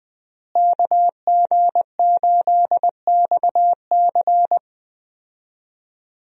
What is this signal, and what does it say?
Morse code, heard as a keyed tone, about 20 wpm, 700 Hz. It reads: KG8XC
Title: USA Call Signs - 5 Characters - Repeated 3x - 20wpm